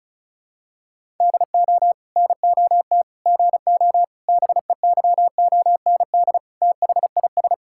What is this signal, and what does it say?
Morse code, heard as a keyed tone, about 34 wpm, 700 Hz. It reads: DO NOT GO BEYOND THIS
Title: Sentences from Top 1000 Words - Review Part 6 - 35wpm